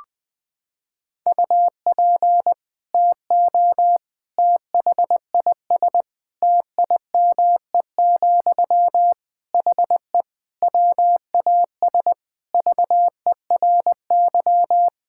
Morse code, heard as a keyed tone, about 20 wpm, 700 Hz. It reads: UP TO THIS TIME, HE WAS VERY
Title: Sentences from Top 1000 Words - Review Part 6 - 20wpm